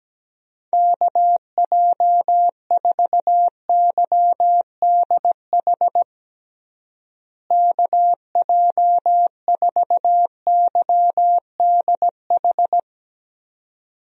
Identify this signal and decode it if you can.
Morse code, heard as a keyed tone, about 17 wpm, 700 Hz. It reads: KJ4YDH KJ4YDH
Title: USA Call Signs - 6 Characters - Repeated 2x - 17wpm